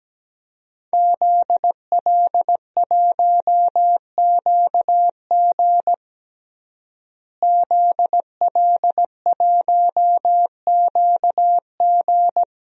Morse code, heard as a keyed tone, about 17 wpm, 700 Hz. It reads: ZL1QG ZL1QG
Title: DXCC-100 Call Signs - 5 Characters - Repeated 2x - 17wpm